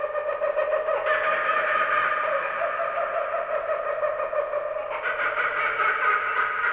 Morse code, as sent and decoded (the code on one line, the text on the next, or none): none